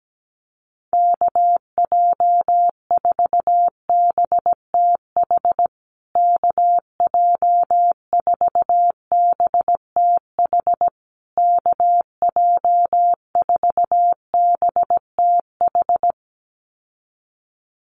Morse code, heard as a keyed tone, about 17 wpm, 700 Hz. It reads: KJ4BTH KJ4BTH KJ4BTH
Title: USA Call Signs Repeated 3x 17wpm